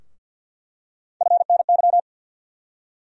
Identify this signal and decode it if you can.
Morse code, heard as a keyed tone, about 46 wpm, 700 Hz. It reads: FNY
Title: USA Call Sign Postfixes - 3 Letters 50wpm